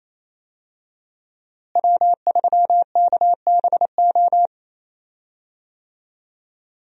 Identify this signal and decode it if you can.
Morse code, heard as a keyed tone, about 28 wpm, 700 Hz. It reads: W3KBO
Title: USA Call Signs - 5 Characters - Repeated 3x - 28wpm